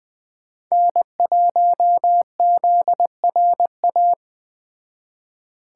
Morse code, heard as a keyed tone, about 20 wpm, 700 Hz. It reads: N1ZRA
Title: Speed Racing - USA 1x3 Call Signs 17wpm